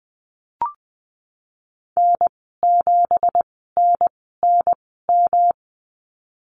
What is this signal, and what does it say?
Morse code, heard as a keyed tone, about 20 wpm, spaced as for 15 wpm, 700 Hz. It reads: N7NNM
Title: 167 Lesson 27g - Number 7 - Call Signs 20wpm